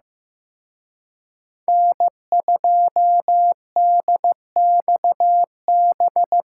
Morse code, heard as a keyed tone, about 18 wpm, 700 Hz. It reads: N2DXB
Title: Speed Racing - USA 1x3 Call Signs 15wpm